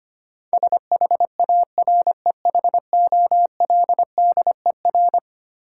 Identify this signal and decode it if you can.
Morse code, heard as a keyed tone, about 25 wpm, 700 Hz. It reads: SHAREHOLDER
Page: morse-code-ninja-practice-25wpm